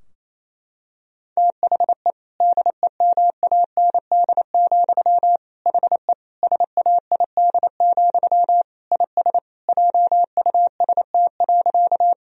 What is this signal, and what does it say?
Morse code, heard as a keyed tone, about 28 wpm, 700 Hz. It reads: THE DEMAND, HE SAID, IS JUST.
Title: Sentences from Top 1000 Words - Review Part 4 - 28wpm